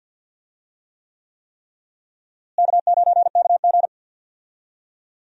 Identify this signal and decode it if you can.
Morse code, heard as a keyed tone, about 54 wpm, 700 Hz. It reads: K9KG
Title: USA Call Signs - 4 Characters - Repeated 3x - 50wpm